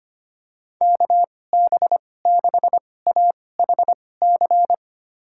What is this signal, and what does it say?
Morse code, heard as a keyed tone, about 25 wpm, spaced as for 18 wpm, 700 Hz. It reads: KB6AHC
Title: USA Call Signs - 6 Characters - 2x Character Spacing - 25wpm